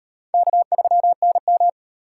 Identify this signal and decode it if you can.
Morse code, heard as a keyed tone, about 34 wpm, 700 Hz. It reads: K3NM